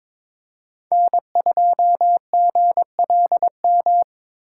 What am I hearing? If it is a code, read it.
Morse code, heard as a keyed tone, about 26 wpm, 700 Hz. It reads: N2GLM
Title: Speed Racing - USA 1x3 Call Signs 22wpm